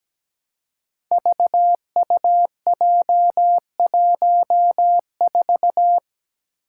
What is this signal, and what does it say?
Morse code, heard as a keyed tone, about 17 wpm, 700 Hz. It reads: VUJ14
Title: Five Challenging Letter-Number Combinations 17wpm